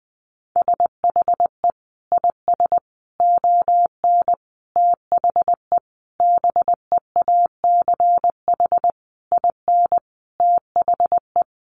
Morse code, heard as a keyed tone, about 20 wpm, 700 Hz. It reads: SHE IS ON THE BEACH IN THE
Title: Sentences from Top 1500 Words 20wpm